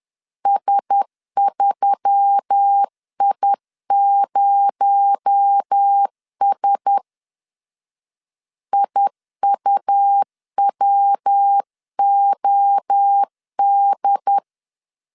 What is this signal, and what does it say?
Morse code, heard as a keyed tone, about 10 wpm, 800 Hz. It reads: S3I0S IUWOD